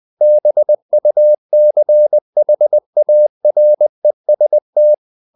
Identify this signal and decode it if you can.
Morse code, heard as a keyed tone, about 20 wpm, 600 Hz. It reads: BUCHAREST
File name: CW / Practice / NationalCapitals20